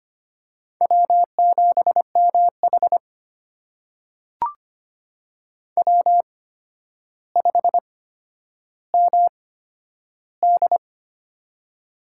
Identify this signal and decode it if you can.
Morse code, heard as a keyed tone, about 25 wpm, spaced as for 11 wpm, 700 Hz. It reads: W7MH W5MD
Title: USA Call Signs - 4 Characters - 8x Character Spacing - 25wpm